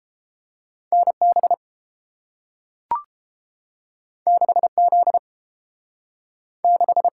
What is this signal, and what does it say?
Morse code, heard as a keyed tone, about 30 wpm, 700 Hz. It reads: NB 6Z 6